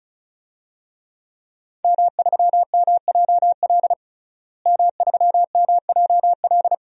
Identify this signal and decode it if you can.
Morse code, heard as a keyed tone, about 34 wpm, 700 Hz. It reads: M3MJL M3MJL
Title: DXCC-100 Call Signs - 5 Characters - Repeated 2x - 35wpm